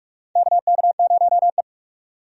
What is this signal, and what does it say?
Morse code, heard as a keyed tone, about 46 wpm, 700 Hz. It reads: KK0E
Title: USA Call Signs - 4 Characters - 45wpm